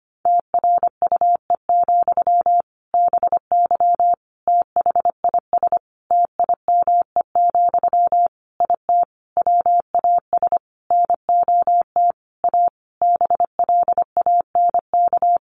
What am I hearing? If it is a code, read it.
Morse code, heard as a keyed tone, about 25 wpm, 700 Hz. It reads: TRUE, BY THIS TIME, IT WAS NOT A BLANK
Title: Sentences from Top 1500 Words 25wpm